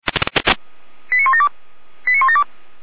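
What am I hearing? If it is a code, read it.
Five-tone selective call (ZVEI), heard as quick runs of five tones, tones of about 70 ms. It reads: 89172 89172
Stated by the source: SDR-received ZVEI five-tone call 89560.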